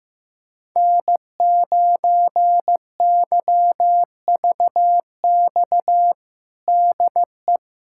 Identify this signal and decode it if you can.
Morse code, heard as a keyed tone, about 15 wpm, 700 Hz. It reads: N9YVX DE
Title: USA Callsign DE Callsign - 5 Characters - 15wpm